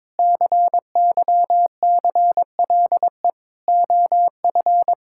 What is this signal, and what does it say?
Morse code, heard as a keyed tone, about 22 wpm, 700 Hz. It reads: CYCLE OF
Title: Sentences from Top 1000 Words - Review Part 6 - 22wpm